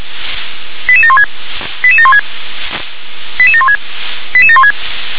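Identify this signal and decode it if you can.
Five-tone selective call (ZVEI), heard as quick runs of five tones, tones of about 70 ms. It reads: 88716 88716 88716 88716
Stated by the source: ZVEI five-tone call received on SDR.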